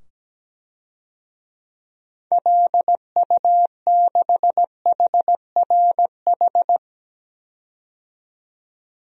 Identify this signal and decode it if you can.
Morse code, heard as a keyed tone, about 17 wpm, 700 Hz. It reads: LU6HRH
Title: DXCC-100 Call Signs - 6 Characters - 17wpm